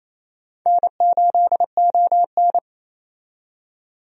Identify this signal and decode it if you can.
Morse code, heard as a keyed tone, about 28 wpm, 700 Hz. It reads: N8ON
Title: USA Call Signs - 4 Characters - 28wpm